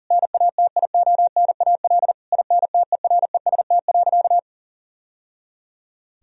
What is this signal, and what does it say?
Morse code, heard as a keyed tone, about 37 wpm, 700 Hz. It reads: NATIONAL INTEREST.
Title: Sentences from Top 1000 Words - Review Part 4 - 40wpm